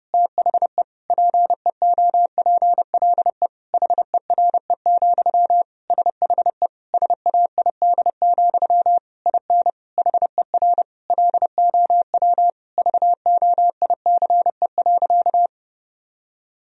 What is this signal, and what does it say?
Morse code, heard as a keyed tone, about 30 wpm, 700 Hz. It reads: THE PEOPLE HERE, SHE SAID, IN HER LOW VOICE.
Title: Sentences from Top 1000 Words - Review Part 1 - 30wpm